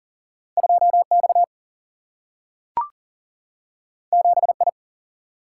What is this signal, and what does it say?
Morse code, heard as a keyed tone, about 42 wpm, 700 Hz. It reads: 2 X 7 I
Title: Single-Letter Single-Number 40wpm